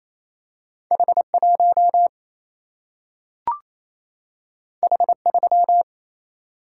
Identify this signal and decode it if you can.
Morse code, heard as a keyed tone, about 28 wpm, 700 Hz. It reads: H 1 H 3
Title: Single-Letter Single-Number 28wpm